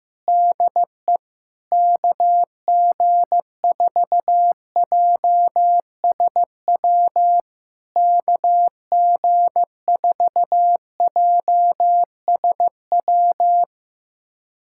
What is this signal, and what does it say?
Morse code, heard as a keyed tone, about 16 wpm, 700 Hz. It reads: DE KG4JSW KG4JSW
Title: SOTA QSO Practice 15wpm